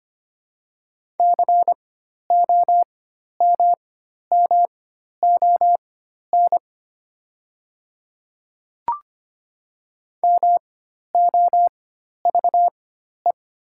Morse code, heard as a keyed tone, about 25 wpm, spaced as for 12 wpm, 700 Hz. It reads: COMMON MOVE
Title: 4x Character Spacing - Top 500 Words - 25wpm